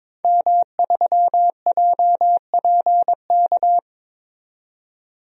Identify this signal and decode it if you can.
Morse code, heard as a keyed tone, about 22 wpm, 700 Hz. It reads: M3JPK
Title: DXCC-100 Call Signs - 5 Characters - 22wpm